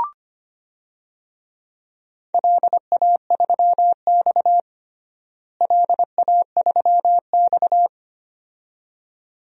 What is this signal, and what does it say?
Morse code, heard as a keyed tone, about 25 wpm, 700 Hz. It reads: LA3X LA3X
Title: DXCC-100 Call Signs - 4 Characters - Repeated 2x - 25wpm